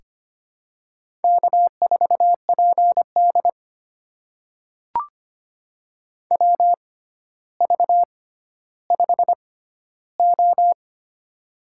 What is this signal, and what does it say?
Morse code, heard as a keyed tone, about 25 wpm, spaced as for 13 wpm, 700 Hz. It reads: K4PD WV5O
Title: USA Call Signs - 4 Characters - 6x Character Spacing - 25wpm